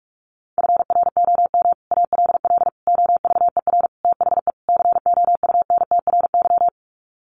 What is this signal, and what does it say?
Morse code, heard as a keyed tone, about 44 wpm, 700 Hz. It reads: FROM ALL OVER THE COUNTRY
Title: Sets of 5 Words Encore 45wpm